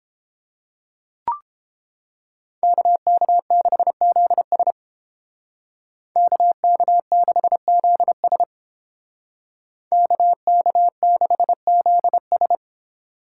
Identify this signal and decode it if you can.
Morse code, heard as a keyed tone, about 29 wpm, 700 Hz. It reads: KK6ZS KK6ZS KK6ZS